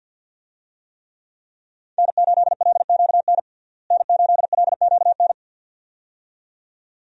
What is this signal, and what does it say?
Morse code, heard as a keyed tone, about 50 wpm, 700 Hz. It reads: N8LQN N8LQN
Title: USA Call Signs - 5 Characters - Repeated 2x - 50wpm